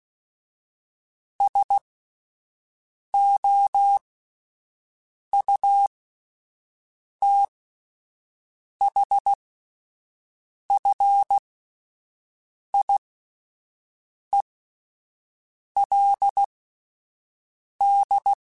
Morse code, heard as a keyed tone, about 16 wpm, spaced as for 5 wpm, 800 Hz. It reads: SOUTHFIELD